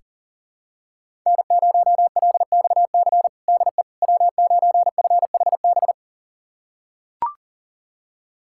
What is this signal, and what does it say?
Morse code, heard as a keyed tone, about 40 wpm, 700 Hz. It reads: N0LXC DE W9FHB
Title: USA Callsign DE Callsign - 5 Characters - 40wpm